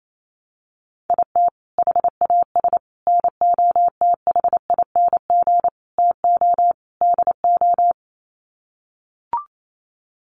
Morse code, heard as a keyed tone, about 28 wpm, 700 Hz. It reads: IT HAS NOTHING TO DO
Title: Sets of 5 Words Encore 28wpm